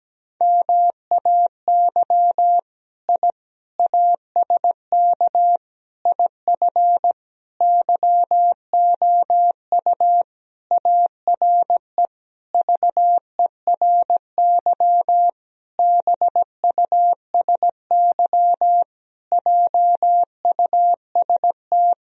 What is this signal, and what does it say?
Morse code, heard as a keyed tone, about 17 wpm, 700 Hz. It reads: MAY I ASK IF YOU ARE VERY BUSY JUST
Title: Sentences from Top 1000 Words - Review Part 3 - 17wpm